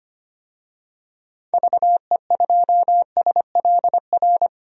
Morse code, heard as a keyed tone, about 25 wpm, 700 Hz. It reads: VE2SLR
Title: DXCC-100 Call Signs - 6 Characters - 25wpm